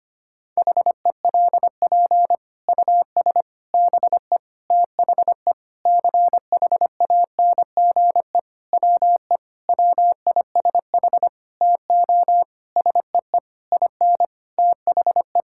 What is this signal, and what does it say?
Morse code, heard as a keyed tone, about 25 wpm, 700 Hz. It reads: HELP US BE THE CHANGE WE WISH TO SEE IN THE
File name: practice / 25wpm / Sentences with 9-Word Phrases - 25wpm